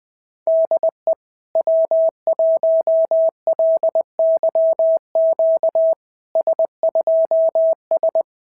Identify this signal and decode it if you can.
Morse code, heard as a keyed tone, about 21 wpm, 650 Hz. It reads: DE W1LYQ S2S
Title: SOTA QSO Practice 20wpm